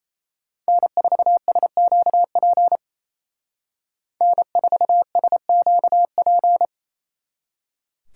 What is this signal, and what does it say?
Morse code, heard as a keyed tone, about 36 wpm, 700 Hz. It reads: N4SQP N4SQP